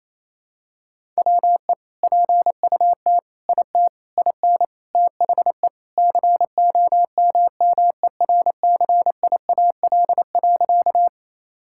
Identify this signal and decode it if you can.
Morse code, heard as a keyed tone, about 28 wpm, 700 Hz. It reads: WE PUT IT IN THE COMMERCIAL.
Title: Sentences from Top 1000 Words - Review Part 5 - 28wpm